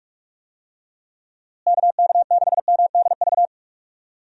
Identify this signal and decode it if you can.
Morse code, heard as a keyed tone, about 45 wpm, 700 Hz. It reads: KK6MDV
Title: USA Call Signs - 6 Characters - Repeated 3x - 45wpm